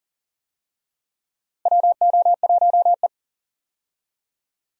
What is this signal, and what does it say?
Morse code, heard as a keyed tone, about 40 wpm, 700 Hz. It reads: WO1E
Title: USA Call Signs - 4 Characters - Repeated 2x - 40wpm